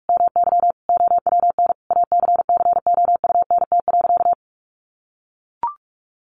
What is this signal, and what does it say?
Morse code, heard as a keyed tone, about 44 wpm, 700 Hz. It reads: MY OWN ACCOUNT.
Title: Sentences from Top 1400 Words 45wpm